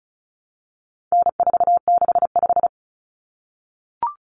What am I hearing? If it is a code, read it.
Morse code, heard as a keyed tone, about 34 wpm, 700 Hz. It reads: N465
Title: Common Cut Numbers - 4-Digit 35wpm